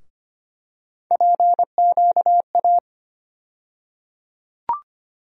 Morse code, heard as a keyed tone, about 25 wpm, 700 Hz. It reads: PQA
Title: USA Call Sign Postfixes - 3 Letters 25wpm